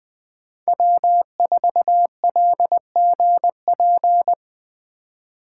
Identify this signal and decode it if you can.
Morse code, heard as a keyed tone, about 20 wpm, 700 Hz. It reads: W4LGP